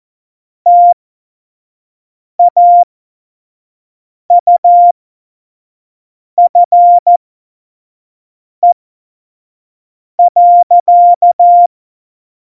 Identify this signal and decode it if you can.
Morse code, heard as a keyed tone, about 14 wpm, spaced as for 5 wpm, 700 Hz. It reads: TAUFE.
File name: Morsekurs / Witze / Witz 01. Dezember, 2020 25 F70